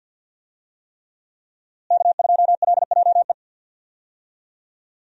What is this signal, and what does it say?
Morse code, heard as a keyed tone, about 48 wpm, 700 Hz. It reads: K2LJE
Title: USA Call Signs - 5 Characters - Repeated 3x - 50wpm